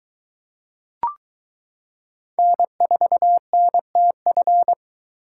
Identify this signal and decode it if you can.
Morse code, heard as a keyed tone, about 22 wpm, 700 Hz. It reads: N4NTF